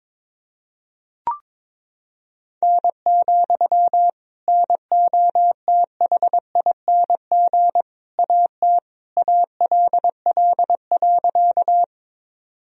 Morse code, heard as a keyed tone, about 22 wpm, 700 Hz. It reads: N, NOTHING AT ALL.
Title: Sentences from Top 500 Words and Abbreviations 22wpm